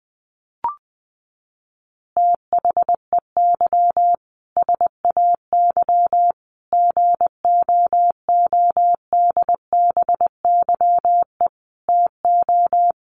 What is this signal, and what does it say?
Morse code, heard as a keyed tone, about 20 wpm, 700 Hz. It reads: THEY SAY GOODBYE TO
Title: Additional Easy Phrases 20wpm